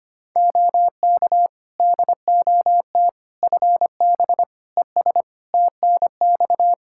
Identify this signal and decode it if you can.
Morse code, heard as a keyed tone, about 25 wpm, 700 Hz. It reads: OK DOT FB ES TNX
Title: Practice QSO - Exchange 2 - Info - 25wpm